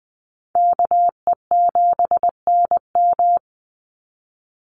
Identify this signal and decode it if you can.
Morse code, heard as a keyed tone, about 20 wpm, 700 Hz. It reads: KE7NM
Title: USA Call Signs 20wpm